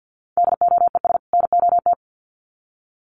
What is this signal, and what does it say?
Morse code, heard as a keyed tone, about 50 wpm, 700 Hz. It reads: DOES NOT
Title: Sets of 2 Words 50wpm